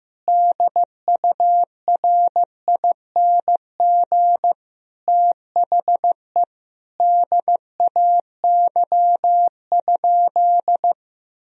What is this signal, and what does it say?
Morse code, heard as a keyed tone, about 15 wpm, 700 Hz. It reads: DURING THE DAY?
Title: Sentences from Top 1000 Words - Review Part 4 - 15wpm